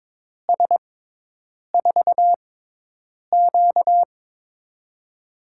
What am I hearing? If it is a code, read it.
Morse code, heard as a keyed tone, about 22 wpm, spaced as for 8 wpm, 700 Hz. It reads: S4Q
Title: USA Call Signs - 6x Character Spacing - 22wpm